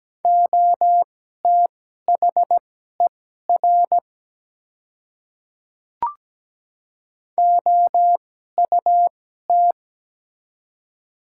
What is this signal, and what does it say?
Morse code, heard as a keyed tone, about 17 wpm, spaced as for 12 wpm, 700 Hz. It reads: OTHER OUT
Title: 2x Character Spacing - Top 200 Words - 17wpm